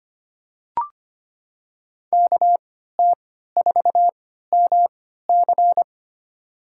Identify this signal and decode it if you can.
Morse code, heard as a keyed tone, about 25 wpm, spaced as for 14 wpm, 700 Hz. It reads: KT4MC
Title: USA Call Signs - 5 Characters - 3x Character Spacing - 25wpm